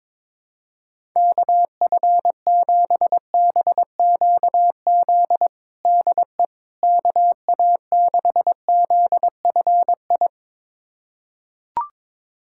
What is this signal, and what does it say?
Morse code, heard as a keyed tone, about 22 wpm, 700 Hz. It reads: KF7BQZ DE KA6ZFI
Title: USA Callsign DE Callsign - 6 Characters - 22wpm